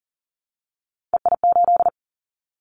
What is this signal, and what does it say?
Morse code, heard as a keyed tone, about 40 wpm, 700 Hz. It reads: EI8
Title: Callsign Prefixes for Top 50 DXCC - 40wpm